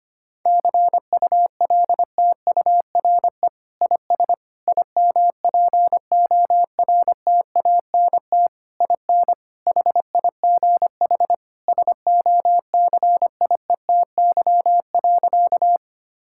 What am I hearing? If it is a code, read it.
Morse code, heard as a keyed tone, about 25 wpm, 700 Hz. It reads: CULTURE IS IMPORTANT IN HIGH SOCIETY.
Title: Sentences from Top 1000 Words - Review Part 6 - 25wpm